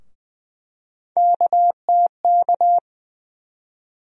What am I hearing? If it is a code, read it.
Morse code, heard as a keyed tone, about 20 wpm, 700 Hz. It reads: KTK
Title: USA Call Sign Postfixes - 3 Letters 20wpm